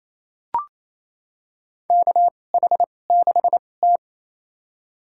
Morse code, heard as a keyed tone, about 28 wpm, spaced as for 20 wpm, 700 Hz. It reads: KH6T
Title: USA Call Signs - 4 Characters - 2x Character Spacing - 28wpm